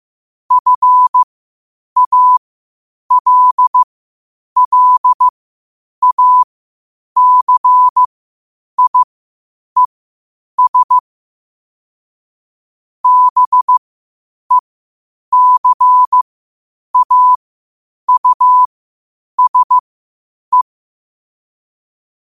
Morse code, heard as a keyed tone, about 15 wpm, spaced as for 8 wpm, 1000 Hz. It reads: FALLACIES BECAUSE
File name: qotdfiles / 8wpm